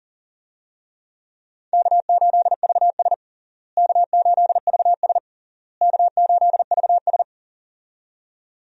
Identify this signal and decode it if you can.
Morse code, heard as a keyed tone, about 37 wpm, 700 Hz. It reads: K8VS K8VS K8VS
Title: USA Call Signs - 4 Characters - Repeated 3x - 40wpm